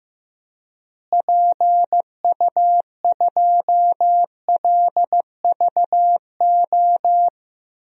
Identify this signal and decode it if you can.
Morse code, heard as a keyed tone, about 15 wpm, 700 Hz. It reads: PU2LVO
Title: DXCC-100 Call Signs - 6 Characters - Repeated 3x - 15wpm